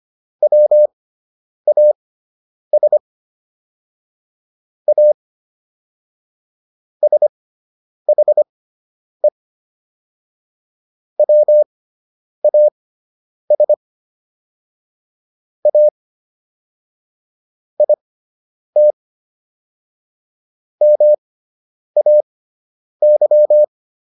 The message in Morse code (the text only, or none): WAS A SHE WAS A IT MAY